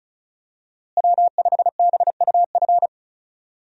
Morse code, heard as a keyed tone, about 34 wpm, 700 Hz. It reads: W5BUF
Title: USA Call Signs - 5 Characters - Repeated 3x - 35wpm